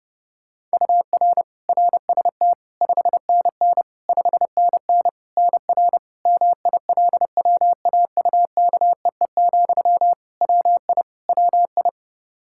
Morse code, heard as a keyed tone, about 32 wpm, 700 Hz. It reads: UR RST 5NN 5NN NR MILWAUKEE, WI WI
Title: Practice QSO - Exchange 1 - Beginning - 30wpm